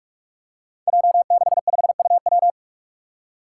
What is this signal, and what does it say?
Morse code, heard as a keyed tone, about 42 wpm, 700 Hz. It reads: J65UW
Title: Five Challenging Letter-Number Combinations 45wpm